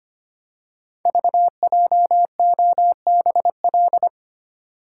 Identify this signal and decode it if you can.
Morse code, heard as a keyed tone, about 25 wpm, 700 Hz. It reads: VJOBL
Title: Code Groups - 5 Letters - 25wpm